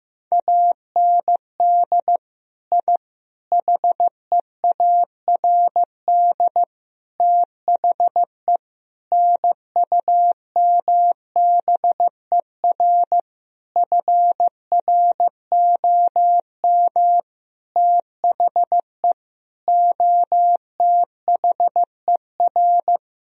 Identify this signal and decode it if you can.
Morse code, heard as a keyed tone, about 15 wpm, 700 Hz. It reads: AND I HEARD THE NUMBER FROM THE OTHER
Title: Sentences from Top 1000 Words - Review Part 1 - 15wpm